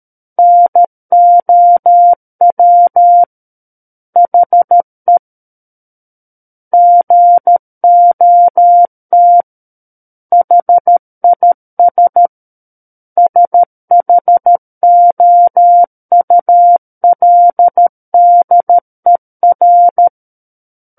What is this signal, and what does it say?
Morse code, heard as a keyed tone, about 13 wpm, 700 Hz. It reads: NOW HE GOT HIS SHOULDER
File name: Book One / 04-Chapter 4 - 13 WPM